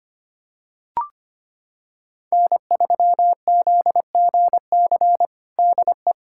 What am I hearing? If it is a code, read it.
Morse code, heard as a keyed tone, about 25 wpm, 700 Hz. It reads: N3ZGC DE
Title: USA Callsign DE Callsign - 5 Characters - 25wpm